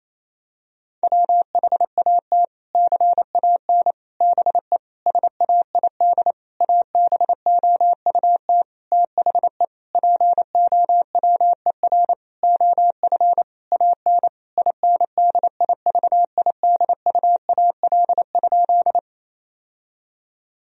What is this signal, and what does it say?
Morse code, heard as a keyed tone, about 28 wpm, 700 Hz. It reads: WHAT CAN BE SAID ABOUT THE POWER OF AN INDIVIDUAL?
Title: Sentences from Top 1000 Words - Review Part 4 - 28wpm